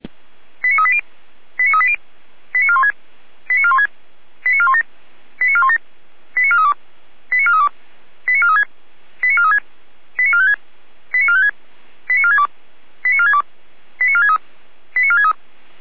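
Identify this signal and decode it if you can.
Five-tone selective call (ZVEI), heard as quick runs of five tones, tones of about 70 ms. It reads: 89380 89380 89416 89416 89417 89417 89432 89432 89436 89436 89456 89456 89462 89462 89463 89463
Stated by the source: Five-tone ZVEI selective call received on an SDR.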